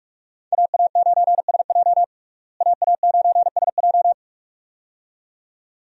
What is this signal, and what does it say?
Morse code, heard as a keyed tone, about 46 wpm, 700 Hz. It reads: AA9SJ AA9SJ
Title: USA Call Signs - 5 Characters - Repeated 2x - 45wpm